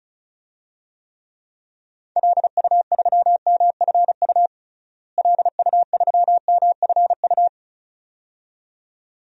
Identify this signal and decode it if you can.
Morse code, heard as a keyed tone, about 34 wpm, 700 Hz. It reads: LU3MFU LU3MFU
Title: DXCC-100 Call Signs - 6 Characters - Repeated 2x - 35wpm